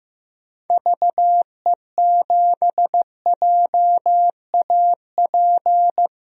Morse code, heard as a keyed tone, about 15 wpm, 700 Hz. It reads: VE7JAP
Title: DXCC-100 Call Signs - 6 Characters - 15wpm